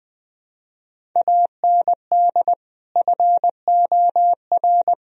Morse code, heard as a keyed tone, about 20 wpm, 700 Hz. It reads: AND FOR
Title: Sentences from Top 1000 Words - Review Part 1 - 20wpm